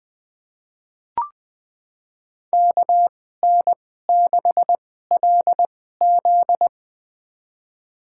Morse code, heard as a keyed tone, about 20 wpm, spaced as for 14 wpm, 700 Hz. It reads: KN6LZ
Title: USA Call Signs - 2x Character Spacing - 20wpm